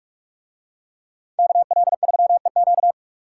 Morse code, heard as a keyed tone, about 44 wpm, 700 Hz. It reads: KL3EQ